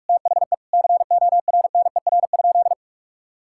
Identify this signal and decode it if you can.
Morse code, heard as a keyed tone, about 42 wpm, 700 Hz. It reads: THE CORNER?
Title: Sentences from Top 1600 Words 45wpm